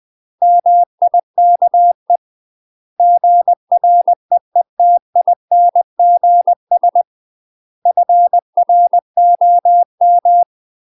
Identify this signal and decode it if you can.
Morse code, heard as a keyed tone, about 20 wpm, 700 Hz. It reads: MIKE GREETINGS FROM
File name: UK MASTER QSOs TRAINING / 20WPM / UK MASTER QSOs 0071